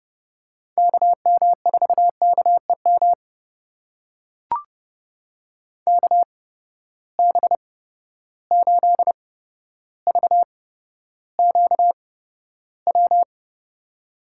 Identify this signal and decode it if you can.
Morse code, heard as a keyed tone, about 31 wpm, spaced as for 13 wpm, 700 Hz. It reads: KM4KEM KB8VQW
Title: USA Call Signs - 6 Characters - 8x Character Spacing - 30wpm